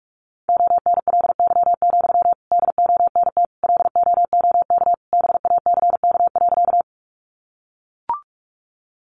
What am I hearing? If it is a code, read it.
Morse code, heard as a keyed tone, about 44 wpm, 700 Hz. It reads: ONLY, DONT LOOK BACK.
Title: Sentences from Top 300 Words 45wpm